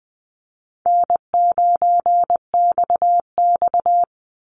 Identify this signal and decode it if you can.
Morse code, heard as a keyed tone, about 20 wpm, 700 Hz. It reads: N9XX
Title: Easy Call Signs Repeated 2x - 20wpm